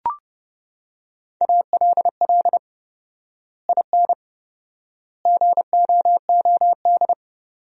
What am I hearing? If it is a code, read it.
Morse code, heard as a keyed tone, about 30 wpm, 700 Hz. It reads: ALL IN GOOD
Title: 4x Word-Spacing - Sentences from Top 200 Words - 30wpm